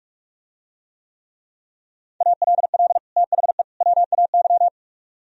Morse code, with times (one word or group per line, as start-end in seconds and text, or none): 2.2-3.0 ALL
3.2-3.6 THE
3.8-4.7 WAY